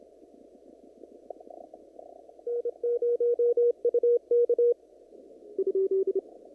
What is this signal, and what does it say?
Morse code, heard as a keyed tone, about 26 wpm, 480 Hz. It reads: N0UK
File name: CW / kt5j wpx cw-op-k5pi